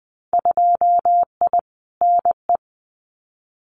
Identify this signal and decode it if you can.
Morse code, heard as a keyed tone, about 20 wpm, 700 Hz. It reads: 2I NE
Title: Winter Field Day Exchanges Encore 20wpm